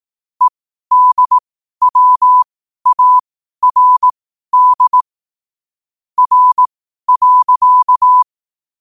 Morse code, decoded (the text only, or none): EDWARD R.